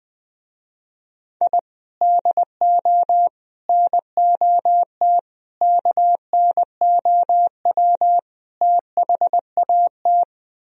Morse code, heard as a keyed tone, about 20 wpm, 700 Hz. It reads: I DO NOT KNOW THAT
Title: Speed-Racing - Sets of 5 Words 17wpm